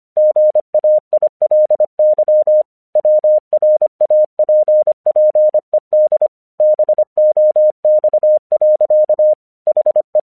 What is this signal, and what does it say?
Morse code, heard as a keyed tone, about 24 wpm, 610 Hz. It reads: GAILY WRAPPED BOX. HE
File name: 1 English 25wpm / AQC19